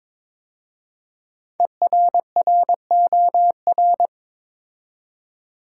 Morse code, heard as a keyed tone, about 22 wpm, 700 Hz. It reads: ERROR
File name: 5 Letter Words - First 3 Letters Given - 22wpm